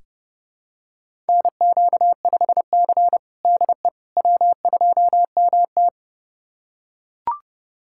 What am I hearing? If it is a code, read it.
Morse code, heard as a keyed tone, about 32 wpm, 700 Hz. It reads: NQ5C DE W2MT
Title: USA Callsign DE Callsign - 4 Characters - 30wpm